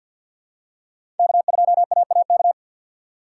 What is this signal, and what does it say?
Morse code, heard as a keyed tone, about 53 wpm, 700 Hz. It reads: K2AAK
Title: USA Call Signs - 5 Characters - 2x Character Spacing - 50wpm